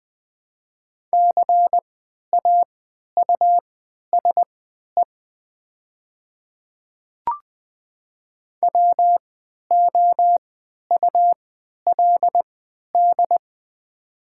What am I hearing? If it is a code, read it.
Morse code, heard as a keyed tone, about 20 wpm, spaced as for 11 wpm, 700 Hz. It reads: CAUSE WOULD
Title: 3x Character Spacing - Top 500 Words - 20wpm